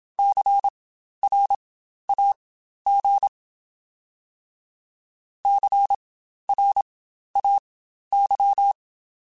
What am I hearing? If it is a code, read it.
Morse code, heard as a keyed tone, about 27 wpm, spaced as for 12 wpm, 800 Hz. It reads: CRAG CRAY